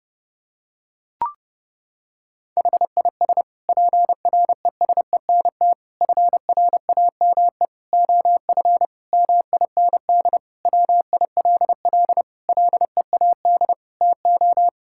Morse code, heard as a keyed tone, about 30 wpm, 700 Hz. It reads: HIS PRESENT FRAME OF MIND WILL LEAD TO
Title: Sentences from Top 1000 Words - Review Part 6 - 30wpm